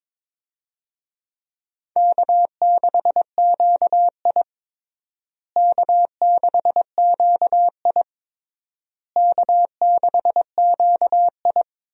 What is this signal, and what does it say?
Morse code, heard as a keyed tone, about 22 wpm, 700 Hz. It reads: K6QI K6QI K6QI
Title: USA Call Signs - 4 Characters - Repeated 3x - 22wpm